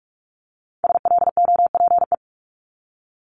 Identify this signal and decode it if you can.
Morse code, heard as a keyed tone, about 46 wpm, 690 Hz. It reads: SLOPE
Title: Multi-letter Phonemes Repeated 3x 45wpm